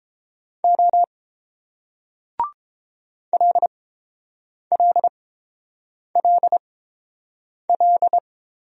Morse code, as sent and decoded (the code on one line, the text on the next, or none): --- / .-.. .-.. .-.. .-..
O LLLL